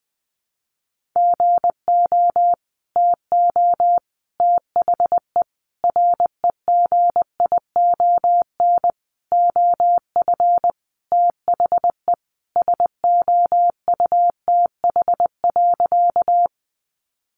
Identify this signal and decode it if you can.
Morse code, heard as a keyed tone, about 20 wpm, 700 Hz. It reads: GO TO THE REGION OF THE SOUTH.
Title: Sentences from Top 1500 Words 20wpm